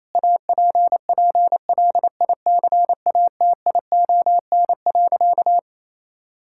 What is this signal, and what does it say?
Morse code, heard as a keyed tone, about 28 wpm, 700 Hz. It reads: APPLICATION.
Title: Sentences from Top 1600 Words 28wpm